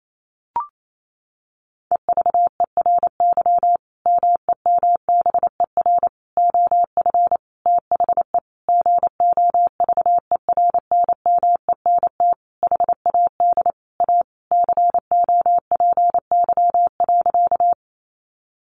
Morse code, heard as a keyed tone, about 28 wpm, 700 Hz. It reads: EVERY MEMBER OF THE GOVERNMENT HAD A COPY.
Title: Sentences from Top 1400 Words 28wpm